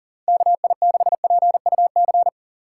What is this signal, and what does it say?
Morse code, heard as a keyed tone, about 38 wpm, 700 Hz. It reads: KI6PUC
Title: USA Call Signs - 6 Characters - 3x Character Spacing - 40wpm